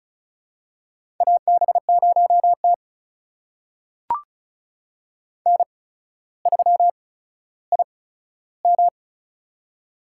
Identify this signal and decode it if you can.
Morse code, heard as a keyed tone, about 36 wpm, spaced as for 15 wpm, 700 Hz. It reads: AB0T N3IM
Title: USA Call Signs - 4 Characters - 8x Character Spacing - 35wpm